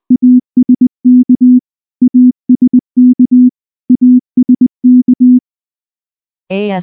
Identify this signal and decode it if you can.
Morse code, heard as a keyed tone, about 19 wpm, 250 Hz. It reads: ASK ASK ASK
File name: morse / morse-training-20WPM-3